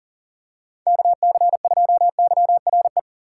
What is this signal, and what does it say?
Morse code, heard as a keyed tone, about 40 wpm, 700 Hz. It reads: KC2YRE
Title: USA Call Signs - 6 Characters - 2x Character Spacing - 40wpm